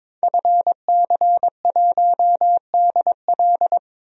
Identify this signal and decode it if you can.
Morse code, heard as a keyed tone, about 22 wpm, 700 Hz. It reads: FC1DL
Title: DXCC-100 Call Signs - 5 Characters - 22wpm